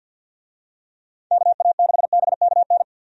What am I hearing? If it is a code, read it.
Morse code, heard as a keyed tone, about 52 wpm, 700 Hz. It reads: KA6BKN
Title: USA Call Signs - 6 Characters - 50wpm